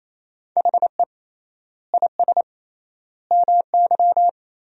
Morse code, heard as a keyed tone, about 28 wpm, 700 Hz. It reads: HE IS MY
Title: 3x Word-Spacing - Sentences from Top 300 Words - 28wpm